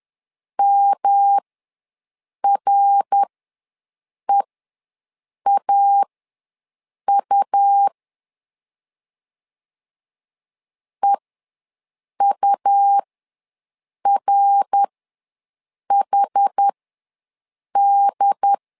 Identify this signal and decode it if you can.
Morse code, heard as a keyed tone, about 11 wpm, spaced as for 6 wpm, 800 Hz. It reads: MREAU EURHD
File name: LEZIONI / 3 MWRO / 3 MWR ws 10wpm 2151